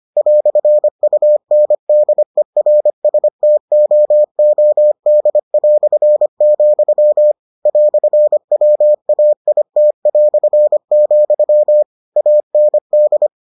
Morse code, heard as a keyed tone, about 25 wpm, 600 Hz. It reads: "UNDERSTOOD", "WAIT", AND